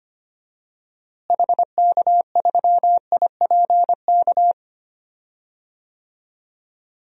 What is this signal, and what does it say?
Morse code, heard as a keyed tone, about 25 wpm, 690 Hz. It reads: HK3IPK
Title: DXCC-100 Call Signs - 6 Characters - Repeated 3x - 25wpm